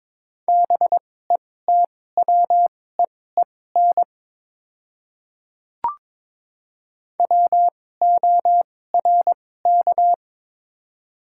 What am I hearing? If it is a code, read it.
Morse code, heard as a keyed tone, about 22 wpm, spaced as for 16 wpm, 700 Hz. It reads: BETWEEN WORK
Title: 2x Character Spacing - Top 300 Words - 22wpm